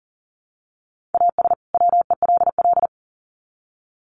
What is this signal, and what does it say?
Morse code, heard as a keyed tone, about 40 wpm, 700 Hz. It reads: AS WELL
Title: Sets of 2 Words 40wpm